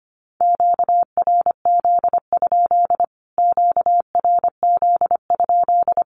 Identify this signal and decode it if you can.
Morse code, heard as a keyed tone, about 25 wpm, 700 Hz. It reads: QRZ? QRZ?
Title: Q-Codes Repeated 3x 25wpm